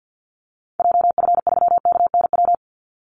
Morse code, heard as a keyed tone, about 49 wpm, 700 Hz. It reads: JF3KNW
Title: Call Signs Repeated 3x - 50wpm